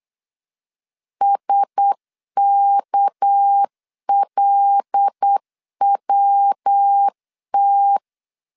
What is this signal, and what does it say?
Morse code, heard as a keyed tone, about 8 wpm, 800 Hz. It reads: SKLWT
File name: LEZIONI / 6 PKX0 / 6 PKX0 8wpm 3341